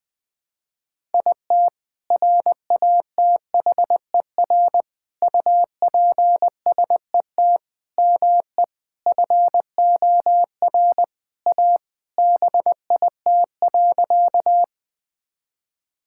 Morse code, heard as a keyed tone, about 21 wpm, 700 Hz. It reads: IT RATHER UPSET ME FOR A BIT.
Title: Sentences from Top 1000 Words - Review Part 5 - 20wpm